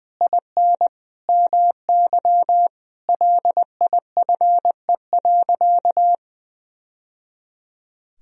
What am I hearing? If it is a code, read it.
Morse code, heard as a keyed tone, about 19 wpm, 700 Hz. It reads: IN MY LIFE.
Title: Sentences from Top 1000 Words - Review Part 3 - 20wpm